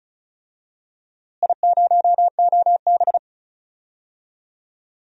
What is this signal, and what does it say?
Morse code, heard as a keyed tone, about 34 wpm, 700 Hz. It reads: I0OB
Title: DXCC-100 Call Signs - 4 Characters - Repeated 3x - 35wpm